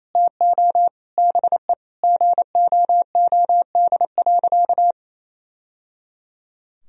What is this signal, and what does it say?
Morse code, heard as a keyed tone, about 27 wpm, 700 Hz. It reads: TO BE GOOD.
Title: Sentences from Top 1000 Words - Review Part 3 - 28wpm